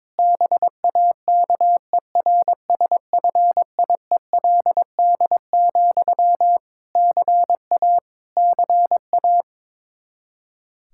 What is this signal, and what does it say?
Morse code, heard as a keyed tone, about 22 wpm, 700 Hz. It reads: BAKERSFIELD, CA CA
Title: Practice QSO - Exchange 1 - Beginning - 22wpm